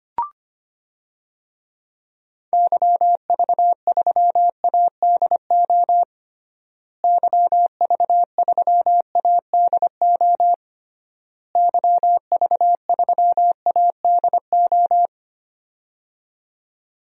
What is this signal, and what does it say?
Morse code, heard as a keyed tone, about 25 wpm, 700 Hz. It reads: YV3ADO YV3ADO YV3ADO
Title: DXCC-100 Call Signs - 6 Characters - Repeated 3x - 25wpm